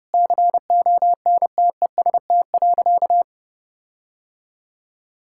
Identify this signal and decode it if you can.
Morse code, heard as a keyed tone, about 32 wpm, 700 Hz. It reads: CONTEST.
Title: Sentences from Top 1600 Words 30wpm